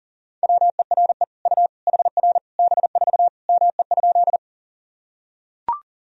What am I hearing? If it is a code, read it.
Morse code, heard as a keyed tone, about 40 wpm, 700 Hz. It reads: WERE U HR B4 ME?
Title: Sentences from Top 200 Words and Abbreviations 40wpm